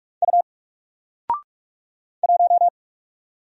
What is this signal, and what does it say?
Morse code, heard as a keyed tone, about 46 wpm, 700 Hz. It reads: U 1
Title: Single Letter-Number 45wpm